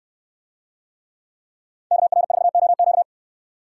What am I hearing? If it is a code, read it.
Morse code, heard as a keyed tone, about 64 wpm, 700 Hz. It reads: KR4OQ